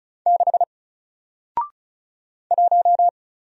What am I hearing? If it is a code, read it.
Morse code, heard as a keyed tone, about 34 wpm, 700 Hz. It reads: 6 1
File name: Single-Letter-Number-35wpm